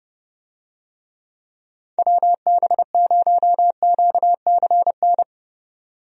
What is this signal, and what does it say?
Morse code, heard as a keyed tone, about 30 wpm, 700 Hz. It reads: WB0QCN
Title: USA Call Signs - 6 Characters - 30wpm